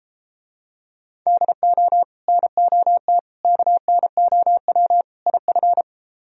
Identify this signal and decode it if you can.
Morse code, heard as a keyed tone, about 32 wpm, 700 Hz. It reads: DO NOT KNOW IF